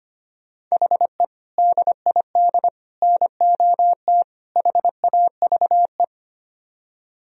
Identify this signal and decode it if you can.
Morse code, heard as a keyed tone, about 32 wpm, 700 Hz. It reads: HE DID NOT HAVE